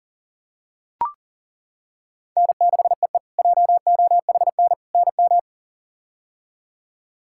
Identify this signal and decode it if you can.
Morse code, heard as a keyed tone, about 38 wpm, 700 Hz. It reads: N6EE JOHN NM
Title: K1USN SST QSO Practice - 40wpm